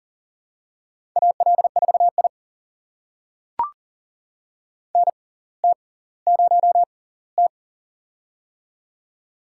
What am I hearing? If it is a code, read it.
Morse code, heard as a keyed tone, about 40 wpm, spaced as for 20 wpm, 700 Hz. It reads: AL4I NT0T
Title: USA Call Signs - 6x Character Spacing - 40wpm